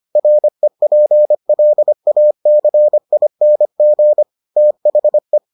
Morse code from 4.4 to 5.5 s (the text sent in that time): THE